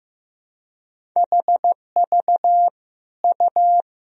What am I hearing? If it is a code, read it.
Morse code, heard as a keyed tone, about 15 wpm, 700 Hz. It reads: HV U
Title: Sentences from Top 400 Words and Abbreviations 15wpm